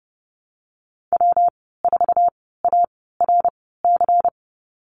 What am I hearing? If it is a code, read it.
Morse code, heard as a keyed tone, about 32 wpm, spaced as for 17 wpm, 700 Hz. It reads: W4ARC
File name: Lesson 37g - Letter x - Call Signs 30wpm